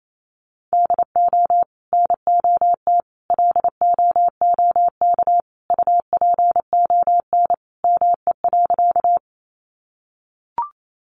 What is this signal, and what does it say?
Morse code, heard as a keyed tone, about 28 wpm, 700 Hz. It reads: DO NOT LOOK UPON ME.
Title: Sentences from Top 1100 Words 28wpm